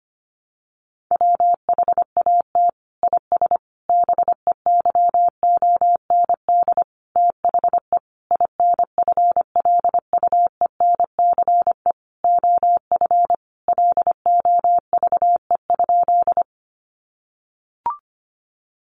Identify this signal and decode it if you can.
Morse code, heard as a keyed tone, about 25 wpm, 700 Hz. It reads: WHAT IS BEYOND THE INFLUENCE OF LOVE?
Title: Sentences from Top 1100 Words 25wpm